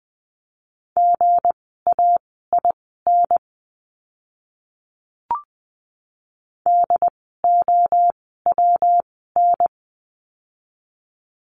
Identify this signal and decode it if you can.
Morse code, heard as a keyed tone, about 20 wpm, spaced as for 15 wpm, 700 Hz. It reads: GAIN DOWN
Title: 160 Lesson 26f - Letter g - Words 20wpm